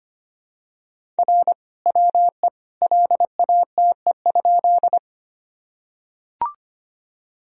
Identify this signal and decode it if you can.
Morse code, heard as a keyed tone, about 25 wpm, 700 Hz. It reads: R WE LATE?
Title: Sentences from Top 300 Words and Abbreviations 25wpm